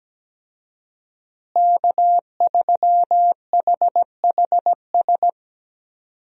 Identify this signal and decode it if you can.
Morse code, heard as a keyed tone, about 17 wpm, 700 Hz. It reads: K3HHS
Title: USA Call Signs - 5 Characters - Repeated 3x - 17wpm